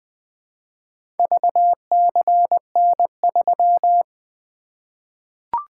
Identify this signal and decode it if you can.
Morse code, heard as a keyed tone, about 21 wpm, 700 Hz. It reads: VCN3
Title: Four Letter-Number Combinations - 20wpm